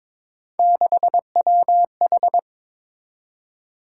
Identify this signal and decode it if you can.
Morse code, heard as a keyed tone, about 22 wpm, 700 Hz. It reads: 6WH
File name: practice / 22wpm / Three Letter-Number Combinations - 22wpm